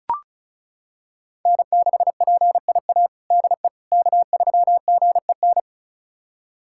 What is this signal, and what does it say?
Morse code, heard as a keyed tone, about 34 wpm, 700 Hz. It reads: N6PIA DE K3GEN
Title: USA Callsign DE Callsign - 5 Characters - 35wpm